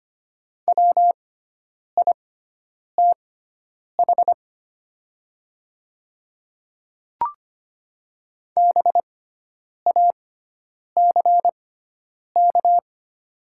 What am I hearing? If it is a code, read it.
Morse code, heard as a keyed tone, about 25 wpm, spaced as for 9 wpm, 700 Hz. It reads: WITH BACK
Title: 6x Character Spacing - Top 500 Words - 25wpm